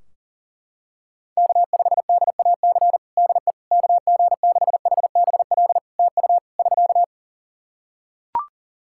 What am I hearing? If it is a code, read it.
Morse code, heard as a keyed tone, about 40 wpm, 700 Hz. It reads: K5DAC DE KG6HBL TU <SK>
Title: K1USN SST QSO Practice 40wpm